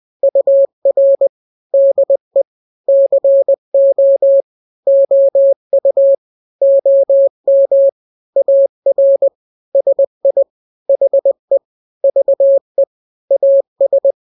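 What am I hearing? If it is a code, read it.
Morse code, heard as a keyed tone, about 20 wpm, 550 Hz. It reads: UR DE CO OU OM AR SI HE VE AS
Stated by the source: Morse drill of two-letter n-grams at 20/18 WPM (Farnsworth), from KK7RR.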